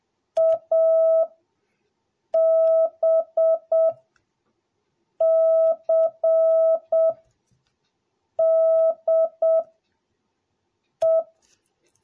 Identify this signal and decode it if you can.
Morse code, heard as a keyed tone, about 7 wpm, spaced as for 4 wpm, 650 Hz. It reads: ABCDE